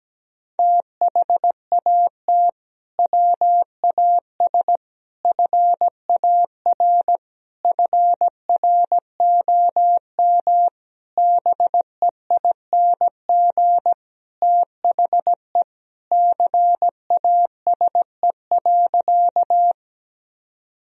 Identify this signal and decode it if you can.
Morse code, heard as a keyed tone, about 17 wpm, 700 Hz. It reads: THAT WAS FAR FROM BEING THE CASE.
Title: Sentences from Top 1000 Words - Review Part 3 - 17wpm